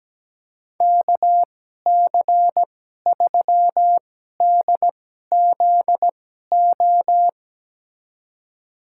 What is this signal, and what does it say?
Morse code, heard as a keyed tone, about 17 wpm, spaced as for 12 wpm, 700 Hz. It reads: KC3DZO
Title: USA Call Signs - 6 Characters - 2x Character Spacing - 17wpm